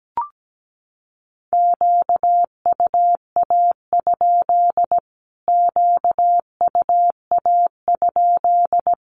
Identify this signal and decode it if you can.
Morse code, heard as a keyed tone, about 17 wpm, 700 Hz. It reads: QUA? QUA?
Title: Q-Codes Repeated 3x 17wpm